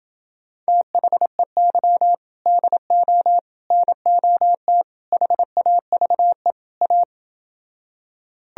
Morse code, heard as a keyed tone, about 30 wpm, 700 Hz. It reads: THEY DO NOT HAVE A